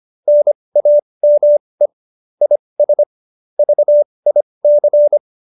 Morse code, heard as a keyed tone, about 25 wpm, spaced as for 20 wpm, 590 Hz. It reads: NAME IS VIC